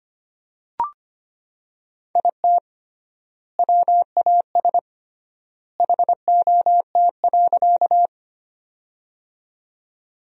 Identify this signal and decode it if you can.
Morse code, heard as a keyed tone, about 25 wpm, 700 Hz. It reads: IT WAS HOT.
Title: 3x Word-Spacing - Sentences from Top 500 Words - 25wpm